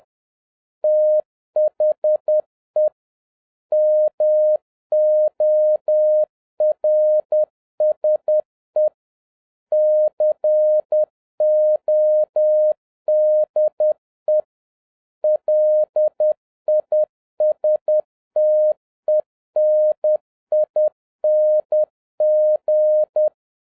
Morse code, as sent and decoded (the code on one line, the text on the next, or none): - .... / -- --- .-. ... / -.-. --- -.. / .-.. .. ... - . -. .. -. --.
THE MORSE CODE LISTENING